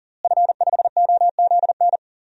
Morse code, heard as a keyed tone, about 40 wpm, 700 Hz. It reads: F5OZN
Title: DXCC-100 Call Signs - 5 Characters - Repeated 3x - 40wpm